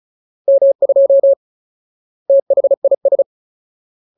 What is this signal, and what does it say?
Morse code, heard as a keyed tone, about 35 wpm, 550 Hz. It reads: M2 THIS
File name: Ultimate Word List / 35WPM Ultimate